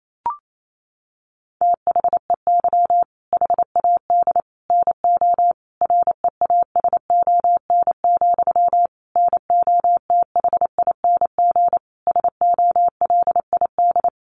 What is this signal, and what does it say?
Morse code, heard as a keyed tone, about 28 wpm, 700 Hz. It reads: THEY HAD NO REASON, NOTHING SOLID
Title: Sentences from Top 1500 Words 28wpm